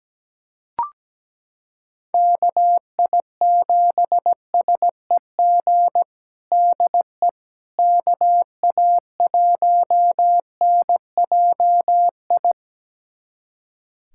Morse code, heard as a keyed tone, about 17 wpm, 700 Hz. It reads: KI7SEG DE KA1NJI
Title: USA Callsign DE Callsign - 6 Characters - 17wpm